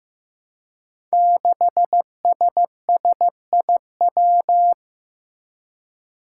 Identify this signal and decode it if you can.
Morse code, heard as a keyed tone, about 15 wpm, 700 Hz. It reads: 6SSIW
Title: Code Groups - 5 Letter:Numbers - 15wpm